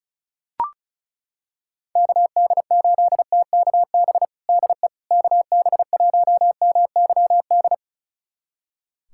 Morse code, heard as a keyed tone, about 36 wpm, 700 Hz. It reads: KD8TKB DE KB1MYD
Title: USA Callsign DE Callsign - 6 Characters - 35wpm